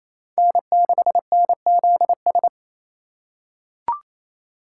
Morse code, heard as a keyed tone, about 28 wpm, 700 Hz. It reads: N6NZS
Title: USA Call Signs - 4x Character Spacing - 28wpm